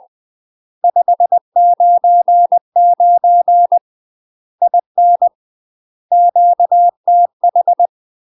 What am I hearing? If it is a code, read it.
Morse code, heard as a keyed tone, about 20 wpm, 700 Hz. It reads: 599 IN QTH
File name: UK MASTER QSOs TRAINING / 20WPM / UK MASTER QSOs 0028